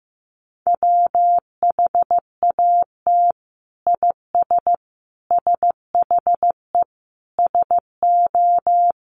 Morse code, heard as a keyed tone, about 15 wpm, 700 Hz. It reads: WHAT IS SHE SO
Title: Sentences from Top 800 Words 15wpm